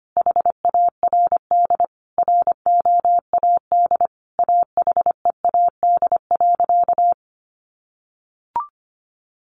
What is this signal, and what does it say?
Morse code, heard as a keyed tone, about 25 wpm, 700 Hz. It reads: HARD ROAD AHEAD.
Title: Sentences from Top 1100 Words 25wpm